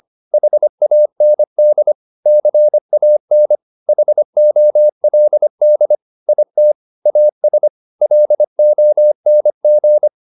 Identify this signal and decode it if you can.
Morse code, heard as a keyed tone, about 25 wpm, 600 Hz. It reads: HAND CAN HOLD IT AS LONG